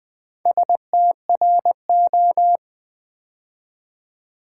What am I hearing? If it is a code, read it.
Morse code, heard as a keyed tone, about 20 wpm, 700 Hz. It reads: STRO
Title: Build A Word - 2 Letters At A Time - 6-Letter Words - 20wpm